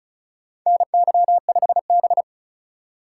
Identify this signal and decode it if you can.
Morse code, heard as a keyed tone, about 36 wpm, 700 Hz. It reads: NY5B
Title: USA Call Signs - 4 Characters - 3x Character Spacing - 35wpm